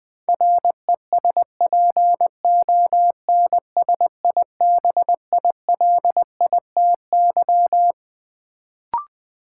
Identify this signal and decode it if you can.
Morse code, heard as a keyed tone, about 20 wpm, 700 Hz. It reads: RESPONSIBILITY
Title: Sentences from Top 1200 Words - New Word Drill - 20wpm